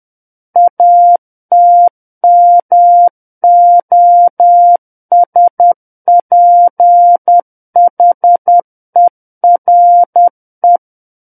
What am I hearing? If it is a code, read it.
Morse code, heard as a keyed tone, about 10 wpm, 700 Hz. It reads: ATMOSPHERE
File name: Book One / 01-Chapter 1 - 10 WPM